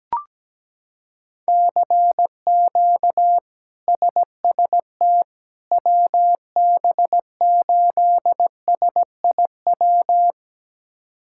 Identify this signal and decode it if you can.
Morse code, heard as a keyed tone, about 17 wpm, 700 Hz. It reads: CQ SST WB8SIW
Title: K1USN SST QSO Practice - 17wpm